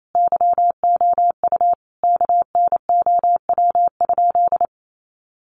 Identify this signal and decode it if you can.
Morse code, heard as a keyed tone, about 28 wpm, 700 Hz. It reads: YOU KNOW?
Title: Sentences from Top 200 Words 28wpm